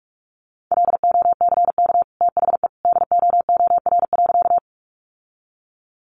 Morse code, heard as a keyed tone, about 44 wpm, 700 Hz. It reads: LOCK THE DOOR.
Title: Sentences from Top 1300 Words 45wpm